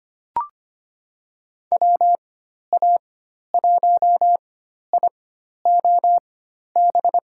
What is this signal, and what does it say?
Morse code, heard as a keyed tone, about 25 wpm, spaced as for 12 wpm, 700 Hz. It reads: WA1IOB
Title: USA Call Signs - 4x Character Spacing - 25wpm